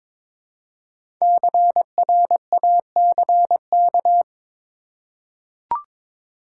Morse code, heard as a keyed tone, about 22 wpm, 700 Hz. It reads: CRACK
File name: 5 Letter Words - First 3 Letters Given - 22wpm